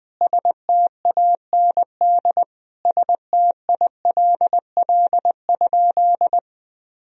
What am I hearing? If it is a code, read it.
Morse code, heard as a keyed tone, about 20 wpm, 700 Hz. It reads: STAND STILL?
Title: Sentences from Top 1000 Words - Review Part 3 - 20wpm